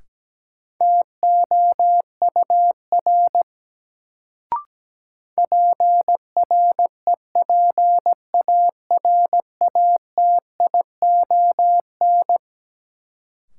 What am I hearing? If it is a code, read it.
Morse code, heard as a keyed tone, about 17 wpm, 700 Hz. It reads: TOUR PREPARATION
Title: Sentences from Top 1400 Words - New Word Drill Alt - 17wpm